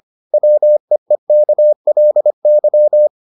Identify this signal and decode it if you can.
Morse code, heard as a keyed tone, about 25 wpm, 600 Hz. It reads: WEEKLY